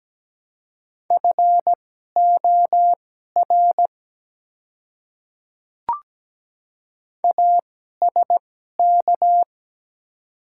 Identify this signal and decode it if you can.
Morse code, heard as a keyed tone, about 17 wpm, spaced as for 12 wpm, 700 Hz. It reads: FOR ASK
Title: 2x Character Spacing - Top 300 Words - 17wpm